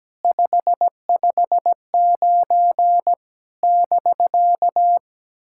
Morse code, heard as a keyed tone, about 17 wpm, 700 Hz. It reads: 559 <BK>
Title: SOTA QSO Practice 17wpm